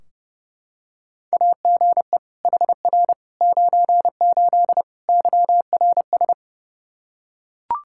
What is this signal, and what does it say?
Morse code, heard as a keyed tone, about 30 wpm, 700 Hz. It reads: AGE HR 98 YRS
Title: K1USN SST QSO Practice 30wpm